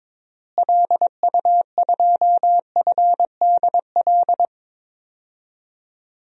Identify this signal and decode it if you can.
Morse code, heard as a keyed tone, about 22 wpm, 700 Hz. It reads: LU2FDL
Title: DXCC-100 Call Signs - 6 Characters - Repeated 3x - 22wpm